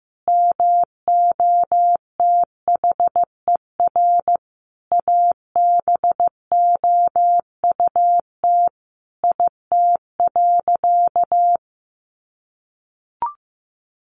Morse code, heard as a keyed tone, about 15 wpm, 700 Hz. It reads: MOTHER ABOUT IT.
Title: Sentences from Top 1100 Words 15wpm